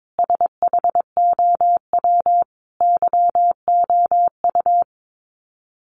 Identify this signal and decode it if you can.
Morse code, heard as a keyed tone, about 22 wpm, 700 Hz. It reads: SHOW YOU
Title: Sets of 5 Words 22wpm